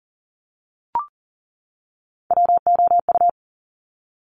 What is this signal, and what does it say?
Morse code, heard as a keyed tone, about 40 wpm, 700 Hz. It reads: WOU
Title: Top Three Letter Combinations in Words 40wpm